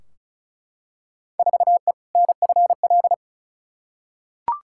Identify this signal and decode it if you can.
Morse code, heard as a keyed tone, about 36 wpm, 690 Hz. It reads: 4E NFL
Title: Field Day Exchanges Encore - 35wpm